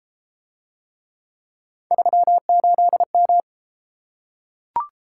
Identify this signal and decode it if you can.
Morse code, heard as a keyed tone, about 29 wpm, 700 Hz. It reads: 38M